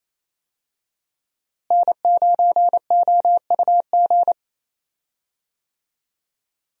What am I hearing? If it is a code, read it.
Morse code, heard as a keyed tone, about 28 wpm, 700 Hz. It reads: N9OUG
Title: USA Call Signs - 5 Characters - Repeated 2x - 28wpm